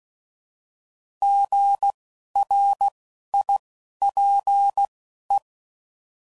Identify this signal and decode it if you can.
Morse code, heard as a keyed tone, about 16 wpm, spaced as for 11 wpm, 800 Hz. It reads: GRIPE
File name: Prefixes / By First Letter